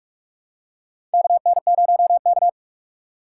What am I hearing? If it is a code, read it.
Morse code, heard as a keyed tone, about 44 wpm, 700 Hz. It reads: KN0K
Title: USA Call Signs - 4 Characters - 3x Character Spacing - 45wpm